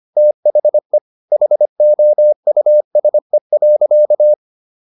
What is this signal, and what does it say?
Morse code, heard as a keyed tone, about 25 wpm, 600 Hz. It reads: THE HOUSE.